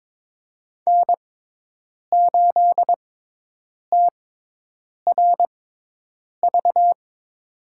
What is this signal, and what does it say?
Morse code, heard as a keyed tone, about 22 wpm, spaced as for 8 wpm, 700 Hz. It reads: N8TRV
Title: USA Call Signs - 6x Character Spacing - 22wpm